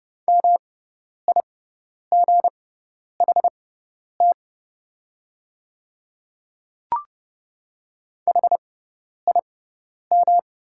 Morse code, heard as a keyed tone, about 30 wpm, spaced as for 10 wpm, 690 Hz. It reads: MIGHT HIM
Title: 6x Character Spacing - Top 500 Words - 30wpm